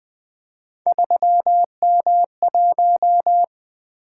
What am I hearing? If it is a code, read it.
Morse code, heard as a keyed tone, about 20 wpm, 700 Hz. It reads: 3M1
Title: Three Letter-Number Combinations - 20wpm